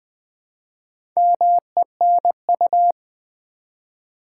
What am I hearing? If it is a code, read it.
Morse code, heard as a keyed tone, about 20 wpm, 700 Hz. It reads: MENU
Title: Easy Words with Hints - 20wpm